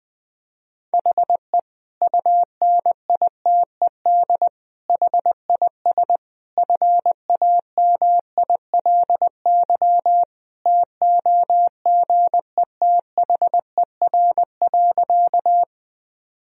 Morse code, heard as a keyed tone, about 20 wpm, 700 Hz. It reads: HE UNITED HIS FAMILY TOGETHER.
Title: Sentences from Top 1000 Words - Review Part 4 - 20wpm